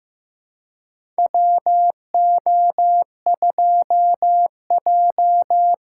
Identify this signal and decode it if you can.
Morse code, heard as a keyed tone, about 15 wpm, 700 Hz. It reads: WO2J
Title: USA Callsign DE Callsign - 4 Characters - 15wpm